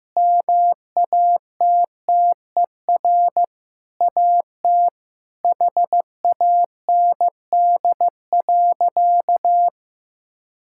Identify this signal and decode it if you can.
Morse code, heard as a keyed tone, about 16 wpm, 700 Hz. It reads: MATTER AT HAND.
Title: Sentences from Top 1600 Words 15wpm